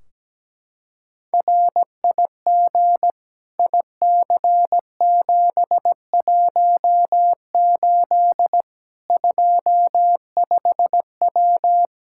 Morse code, heard as a keyed tone, about 17 wpm, 700 Hz. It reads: RIG IC718 25W
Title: Practice QSO - Exchange 2 - Info - 17wpm